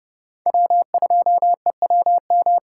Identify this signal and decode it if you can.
Morse code, heard as a keyed tone, about 29 wpm, 700 Hz. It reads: W2EWM